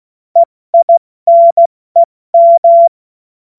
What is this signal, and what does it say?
Morse code, heard as a keyed tone, about 16 wpm, 680 Hz. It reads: EINEM
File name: Morsekurs / Witze / Witz 14.November 2023 70 F80